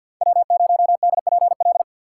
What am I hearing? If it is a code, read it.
Morse code, heard as a keyed tone, about 51 wpm, 700 Hz. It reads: W0DPL
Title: USA Call Signs - 5 Characters - 4x Character Spacing - 50wpm